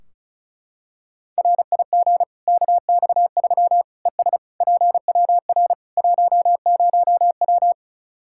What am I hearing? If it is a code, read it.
Morse code, heard as a keyed tone, about 36 wpm, 700 Hz. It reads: RIG KX3 ES PWR 10W
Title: Practice QSO - Exchange 2 - Info - 35wpm